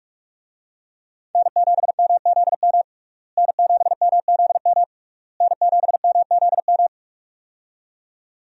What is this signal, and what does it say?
Morse code, heard as a keyed tone, about 46 wpm, 700 Hz. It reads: N7MZM N7MZM N7MZM
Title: USA Call Signs - 5 Characters - Repeated 3x - 45wpm